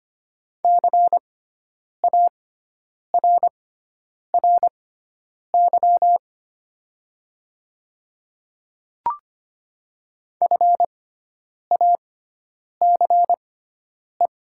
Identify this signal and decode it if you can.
Morse code, heard as a keyed tone, about 25 wpm, spaced as for 9 wpm, 700 Hz. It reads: CARRY FACE
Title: 6x Character Spacing - Top 300 Words - 25wpm